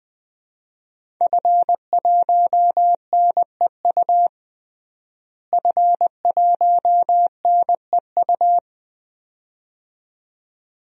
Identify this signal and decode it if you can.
Morse code, heard as a keyed tone, about 21 wpm, 700 Hz. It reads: F1NEU F1NEU
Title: DXCC-100 Call Signs - 5 Characters - Repeated 2x - 20wpm